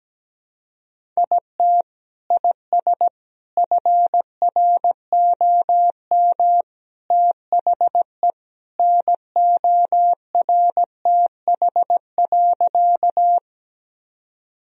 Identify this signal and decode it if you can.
Morse code, heard as a keyed tone, about 17 wpm, 700 Hz. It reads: IT IS FROM THE NORTH.
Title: Sentences from Top 1000 Words - Review Part 1 - 17wpm